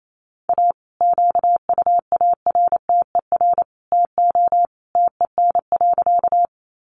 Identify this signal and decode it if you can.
Morse code, heard as a keyed tone, about 28 wpm, 700 Hz. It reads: A QUARTER TO TEN.
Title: Sentences from Top 1300 Words 28wpm